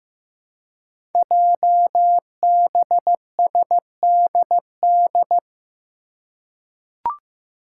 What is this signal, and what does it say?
Morse code, heard as a keyed tone, about 15 wpm, 700 Hz. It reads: JBSDD
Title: Five Challenging Letter-Number Combinations 15wpm